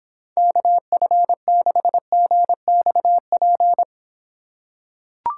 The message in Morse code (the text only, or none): KF6GXP